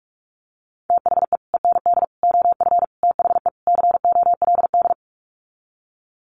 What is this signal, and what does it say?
Morse code, heard as a keyed tone, about 44 wpm, 700 Hz. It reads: THE END OF THE COLD
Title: Sets of 5 Words 45wpm